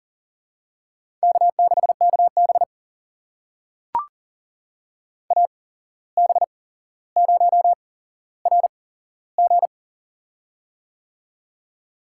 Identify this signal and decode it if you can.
Morse code, heard as a keyed tone, about 40 wpm, spaced as for 16 wpm, 700 Hz. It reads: K6KB AB0RG
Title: USA Call Signs - 8x Character Spacing - 40wpm